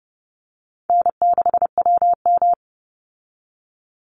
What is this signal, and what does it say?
Morse code, heard as a keyed tone, about 30 wpm, 700 Hz. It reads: N6WM
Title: Call Signs Repeated 3x - 30wpm